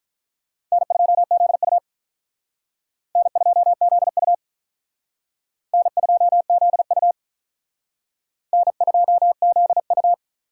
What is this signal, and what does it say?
Morse code, heard as a keyed tone, about 45 wpm, 700 Hz. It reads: N2ZU N2ZU N2ZU N2ZU